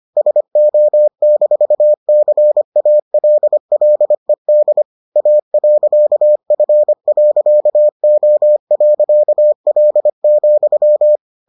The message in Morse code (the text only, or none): SO-CALLED A.F.O.L,